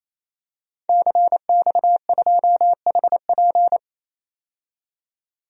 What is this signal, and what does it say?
Morse code, heard as a keyed tone, about 28 wpm, 700 Hz. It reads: CX2HP
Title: DXCC-100 Call Signs - 5 Characters - 28wpm